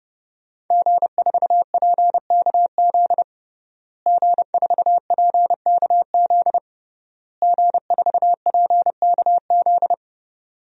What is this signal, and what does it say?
Morse code, heard as a keyed tone, about 30 wpm, 700 Hz. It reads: G4PKZ G4PKZ G4PKZ
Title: DXCC-100 Call Signs - 5 Characters - Repeated 3x - 30wpm